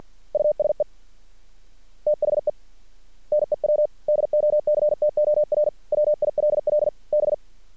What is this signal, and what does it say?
Morse code, heard as a keyed tone, about 50 wpm, 600 Hz. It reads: USE THE NEW DOCTOR WILL B
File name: archived / CRSnet